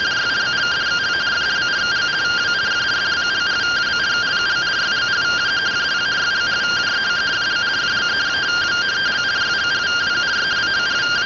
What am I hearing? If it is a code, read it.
Two-tone teleprinter signal, about 170 Hz shift.